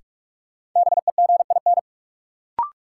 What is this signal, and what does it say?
Morse code, heard as a keyed tone, about 46 wpm, 700 Hz. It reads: BEGIN
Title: Sentences from Top 300 Words - New Word Drill - 45wpm